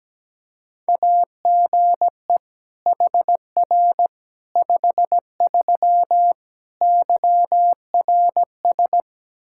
Morse code, heard as a keyed tone, about 17 wpm, 700 Hz. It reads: AGE HR 53 YRS
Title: Practice QSO - Exchange 3 - More Info - 17wpm